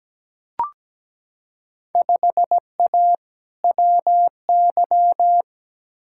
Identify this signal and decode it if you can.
Morse code, heard as a keyed tone, about 17 wpm, 700 Hz. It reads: 5A WY
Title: Field Day Exchanges Encore - 17wpm